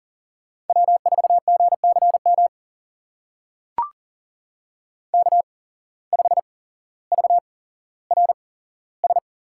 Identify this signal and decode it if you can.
Morse code, heard as a keyed tone, about 37 wpm, spaced as for 17 wpm, 700 Hz. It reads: W4GCM K5VRS
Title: USA Call Signs - 5 Characters - 8x Character Spacing - 40wpm